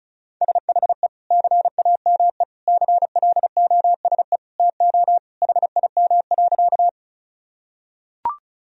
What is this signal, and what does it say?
Morse code, heard as a keyed tone, about 34 wpm, 700 Hz. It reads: SHE CAME CLOSE TO HIM.
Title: Sentences from Top 1000 Words - Review Part 1 - 35wpm